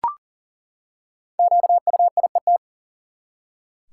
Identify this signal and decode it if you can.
Morse code, heard as a keyed tone, about 40 wpm, 700 Hz. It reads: QUIET
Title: Five Letter Words - 40wpm